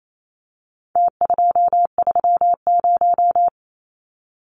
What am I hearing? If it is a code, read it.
Morse code, heard as a keyed tone, about 28 wpm, 700 Hz. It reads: T230
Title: Numbers - Contest Style - 4-Digit 28wpm